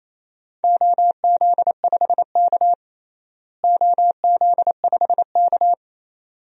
Morse code, heard as a keyed tone, about 28 wpm, 700 Hz. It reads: OZ5K OZ5K
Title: DXCC-100 Call Signs - 4 Characters - Repeated 2x - 28wpm